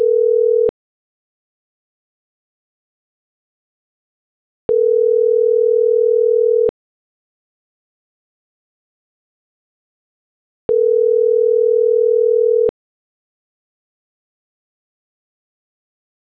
Ringback tone (North America), 440 + 480 Hz.